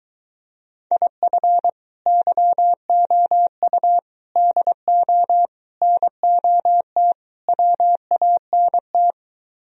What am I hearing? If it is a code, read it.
Morse code, heard as a keyed tone, about 23 wpm, 700 Hz. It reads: IF YOU DO NOT WANT